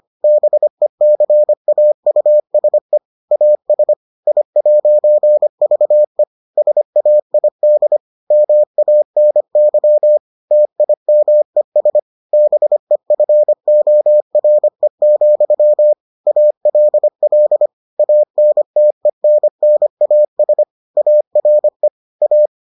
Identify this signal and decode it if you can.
Morse code, heard as a keyed tone, about 25 wpm, 600 Hz. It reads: BECAUSE AS I'VE SAID MANY TIMES BEFORE, ALL ANTENNAS ARE A